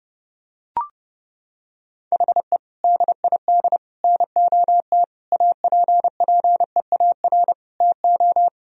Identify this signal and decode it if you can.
Morse code, heard as a keyed tone, about 30 wpm, 700 Hz. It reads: HE DID NOT APPEAR TO
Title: Sentences from Top 1000 Words - Review Part 2 - 30wpm